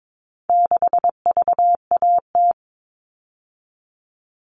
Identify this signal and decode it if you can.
Morse code, heard as a keyed tone, about 22 wpm, 700 Hz. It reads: 6VAT
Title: All Cut Numbers - 4-Digit 22wpm